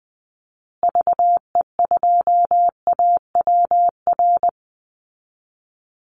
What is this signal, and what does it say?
Morse code, heard as a keyed tone, about 20 wpm, 700 Hz. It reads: VE2AWR
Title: Call Signs Repeated 3x - 20wpm